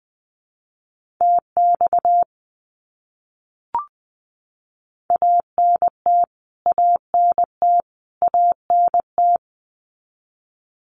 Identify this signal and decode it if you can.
Morse code, heard as a keyed tone, about 20 wpm, 700 Hz. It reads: TX ANT ANT ANT
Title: Abbreviations Repeated 3x - 20wpm